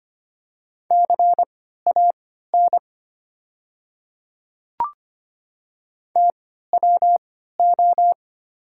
Morse code, heard as a keyed tone, about 25 wpm, spaced as for 14 wpm, 700 Hz. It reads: CAN TWO
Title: 3x Character Spacing - Top 100 Words - 25wpm